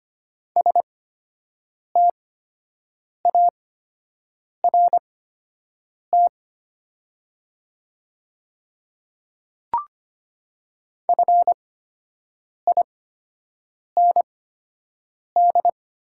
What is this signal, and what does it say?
Morse code, heard as a keyed tone, about 25 wpm, spaced as for 7 wpm, 700 Hz. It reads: START FIND
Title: 8x Character Spacing - Top 300 Words - 25wpm